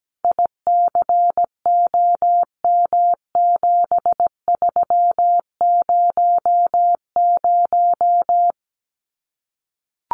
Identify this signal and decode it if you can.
Morse code, heard as a keyed tone, about 17 wpm, 700 Hz. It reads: ICOM7300
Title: Common QSO Phrases 17wpm